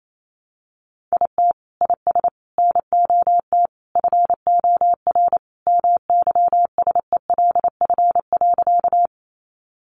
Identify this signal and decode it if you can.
Morse code, heard as a keyed tone, about 28 wpm, 700 Hz. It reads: IT IS NOT FOR MYSELF.
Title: Sentences from Top 1300 Words 28wpm